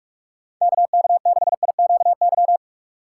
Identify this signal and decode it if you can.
Morse code, heard as a keyed tone, about 46 wpm, 700 Hz. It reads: KK6IQY
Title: USA Call Signs - 6 Characters - 3x Character Spacing - 45wpm